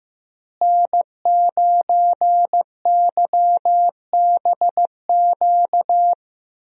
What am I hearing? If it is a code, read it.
Morse code, heard as a keyed tone, about 15 wpm, 700 Hz. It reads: N9YBQ
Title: USA Call Signs - 5 Characters - Repeated 3x - 15wpm